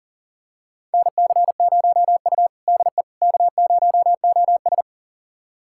Morse code, heard as a keyed tone, about 37 wpm, 700 Hz. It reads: NC0U DE K0OS
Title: USA Callsign DE Callsign - 4 Characters - 40wpm